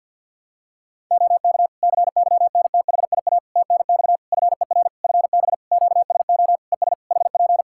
Morse code, heard as a keyed tone, about 52 wpm, 700 Hz. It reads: OK CYNTHIA TNX FER FB QSO ES HP
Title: K1USN SST QSO Practice 50wpm